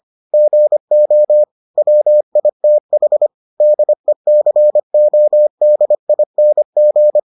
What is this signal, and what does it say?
Morse code, heard as a keyed tone, about 25 wpm, 600 Hz. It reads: GO WITH DECODING